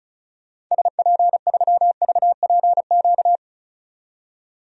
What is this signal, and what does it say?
Morse code, heard as a keyed tone, about 34 wpm, 700 Hz. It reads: SP3VPQ
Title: DXCC-100 Call Signs - 6 Characters - 35wpm